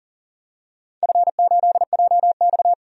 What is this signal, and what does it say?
Morse code, heard as a keyed tone, about 40 wpm, 700 Hz. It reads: F8JX
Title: DXCC-100 Call Signs - 4 Characters - 40wpm